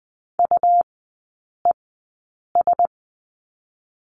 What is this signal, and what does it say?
Morse code, heard as a keyed tone, about 20 wpm, spaced as for 8 wpm, 700 Hz. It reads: UES
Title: 100 Lesson 17e - Letter u - Three Characters 20wpm